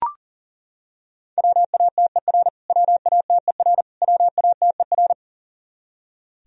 Morse code, heard as a keyed tone, about 40 wpm, 700 Hz. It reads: WATER WATER WATER
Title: Top 100 Words Repeated 40wpm